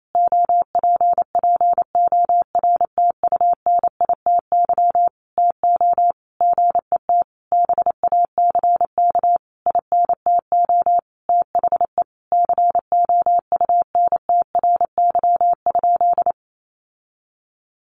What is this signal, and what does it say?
Morse code, heard as a keyed tone, about 28 wpm, 700 Hz. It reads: OPPORTUNITY TO GET BACK INTO THE COUNTRY?
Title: Sentences from Top 1400 Words 28wpm